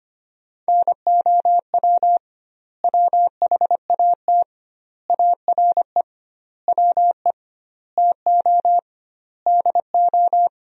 Morse code, heard as a keyed tone, about 25 wpm, 700 Hz. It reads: NOW WHAT ARE WE TO DO
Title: 2x Word-Spacing - Sentences from Top 100 Words - 25wpm